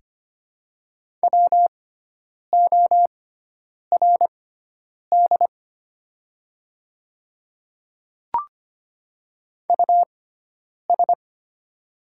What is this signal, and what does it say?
Morse code, heard as a keyed tone, about 25 wpm, spaced as for 9 wpm, 700 Hz. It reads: WORD US
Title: 6x Character Spacing - Top 300 Words - 25wpm